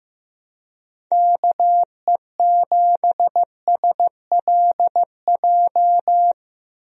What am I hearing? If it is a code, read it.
Morse code, heard as a keyed tone, about 15 wpm, 700 Hz. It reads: KE7SLJ
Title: USA Call Signs - 6 Characters - Repeated 3x - 15wpm